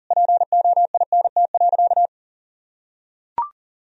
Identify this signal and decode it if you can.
Morse code, heard as a keyed tone, about 40 wpm, 700 Hz. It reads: POINT.
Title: Sentences from Top 1000 Words - Review Part 1 - 40wpm